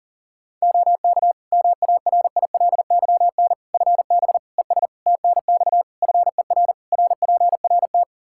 Morse code, heard as a keyed tone, about 40 wpm, 700 Hz. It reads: OK MARILYN FB ES TNX FER RPRT
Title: Practice QSO - Exchange 2 - Info - 40wpm